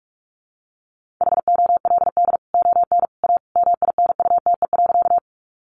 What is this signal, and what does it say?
Morse code, heard as a keyed tone, about 44 wpm, 700 Hz. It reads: HOLD ON A MINUTE.
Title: Sentences from Top 500 Words 45wpm